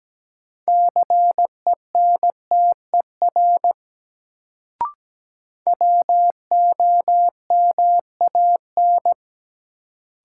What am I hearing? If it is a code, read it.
Morse code, heard as a keyed tone, about 17 wpm, 700 Hz. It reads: CENTER WOMAN
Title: Sentences from Top 1000 Words - Review Part 2 - New Word Drill - 17wpm